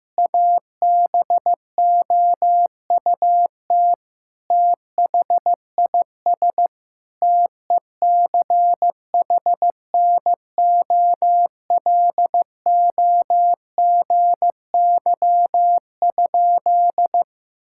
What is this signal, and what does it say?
Morse code, heard as a keyed tone, about 15 wpm, 700 Hz. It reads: ABOUT THIS TECHNOLOGY?
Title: Sentences from Top 1000 Words - Review Part 5 - 15wpm